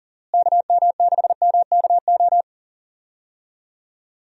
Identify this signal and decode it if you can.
Morse code, heard as a keyed tone, about 40 wpm, 700 Hz. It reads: KM6MKO
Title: USA Call Signs - 6 Characters - Repeated 3x - 40wpm